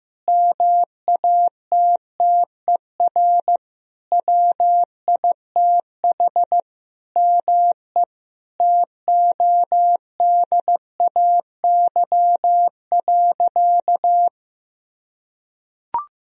Morse code, heard as a keyed tone, about 16 wpm, 700 Hz. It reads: MATTER WITH ME TODAY.
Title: Sentences from Top 1000 Words - Review Part 4 - 15wpm